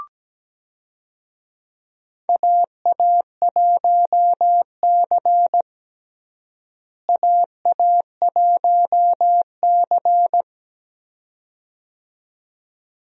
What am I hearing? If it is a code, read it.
Morse code, heard as a keyed tone, about 17 wpm, 700 Hz. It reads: AA1C AA1C
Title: USA Call Signs - 4 Characters - Repeated 2x - 17wpm